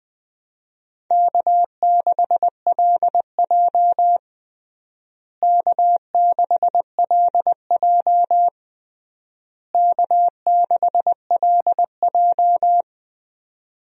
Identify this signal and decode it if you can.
Morse code, heard as a keyed tone, about 20 wpm, 700 Hz. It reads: K6LJ K6LJ K6LJ
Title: USA Call Signs - 4 Characters - Repeated 3x - 20wpm